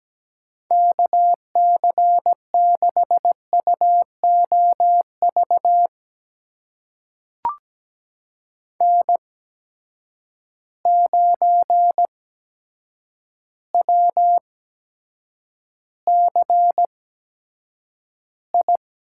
Morse code, heard as a keyed tone, about 17 wpm, spaced as for 8 wpm, 700 Hz. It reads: KC6UOV N9WCI
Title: USA Call Signs - 8x Character Spacing - 17wpm